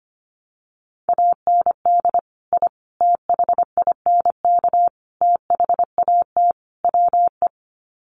Morse code, heard as a keyed tone, about 25 wpm, 700 Hz. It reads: AND I THINK THAT WE
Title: Sets of 5 Words Encore2 25wpm